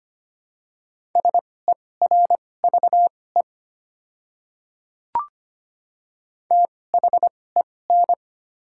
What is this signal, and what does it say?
Morse code, heard as a keyed tone, about 25 wpm, spaced as for 18 wpm, 700 Hz. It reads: SERVE THEN
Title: 2x Character Spacing - Top 500 Words - 25wpm